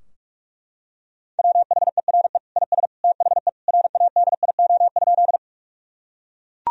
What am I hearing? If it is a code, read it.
Morse code, heard as a keyed tone, about 41 wpm, 700 Hz. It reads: WHERE IS THE RADIO?
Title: Sentences from Top 1000 Words - Review Part 3 - 45wpm